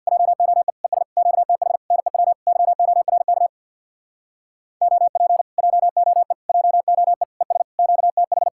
Morse code, heard as a keyed tone, about 62 wpm, 700 Hz. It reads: JOE ES QTH NEW YORK OP JOE JOE ES QTH
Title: Speed-Racing Common QSO Phrases - 50wpm